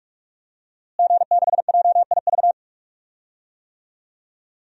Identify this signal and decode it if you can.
Morse code, heard as a keyed tone, about 46 wpm, 700 Hz. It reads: G6JIV
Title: DXCC-100 Call Signs - 5 Characters - Repeated 3x - 45wpm